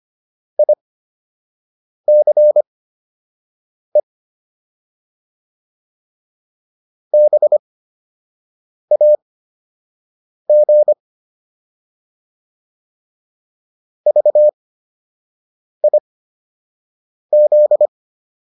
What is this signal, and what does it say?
Morse code, heard as a keyed tone, about 25 wpm, spaced as for 6 wpm, 600 Hz. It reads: ICE BAG VIZ